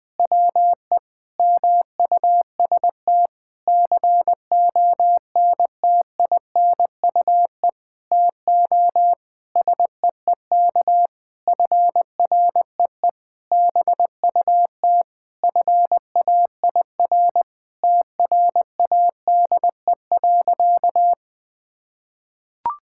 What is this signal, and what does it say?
Morse code, heard as a keyed tone, about 21 wpm, 700 Hz. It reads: WE MUST CONTINUE TO SEEK FREE BUT FAIR TRADE.
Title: Sentences from Top 1600 Words 20wpm